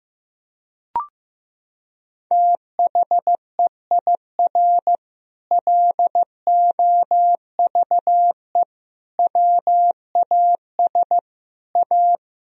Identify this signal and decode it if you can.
Morse code, heard as a keyed tone, about 15 wpm, 700 Hz. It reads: THEIR LOVE WAS A
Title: Sentences from Top 1000 Words - Review Part 4 - 15wpm